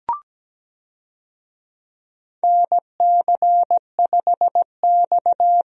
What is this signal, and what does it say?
Morse code, heard as a keyed tone, about 17 wpm, 700 Hz. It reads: NC5X
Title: USA Call Signs - 4 Characters - Repeated 3x - 17wpm